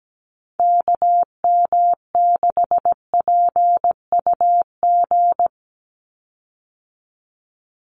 Morse code, heard as a keyed tone, about 17 wpm, 700 Hz. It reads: KM6PUG
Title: USA Call Signs 17wpm